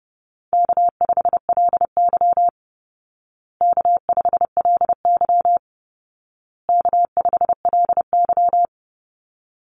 Morse code, heard as a keyed tone, about 30 wpm, 700 Hz. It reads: K5LY K5LY K5LY
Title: Easy Call Signs Repeated 3x - 30wpm